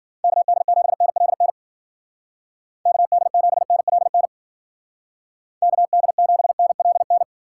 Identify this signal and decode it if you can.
Morse code, heard as a keyed tone, about 50 wpm, 700 Hz. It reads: KD7NLN KD7NLN KD7NLN
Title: Speed Racing - USA 2x3 Call Signs 40wpm